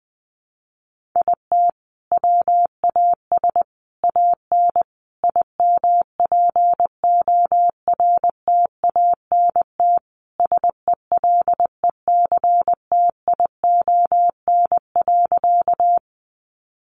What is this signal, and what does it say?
Morse code, heard as a keyed tone, about 20 wpm, 700 Hz. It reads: IT WAS AN IMPORTANT SELECTION.
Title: Sentences from Top 1400 Words 20wpm